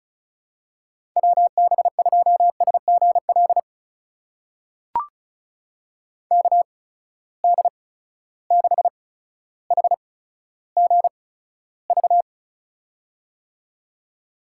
Morse code, heard as a keyed tone, about 36 wpm, spaced as for 15 wpm, 700 Hz. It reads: WB2SGL KD6HGV
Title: USA Call Signs - 6 Characters - 8x Character Spacing - 35wpm